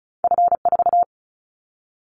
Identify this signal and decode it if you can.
Morse code, heard as a keyed tone, about 34 wpm, 700 Hz. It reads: F4
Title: Callsign Prefixes for Top 25 DXCC - 35wpm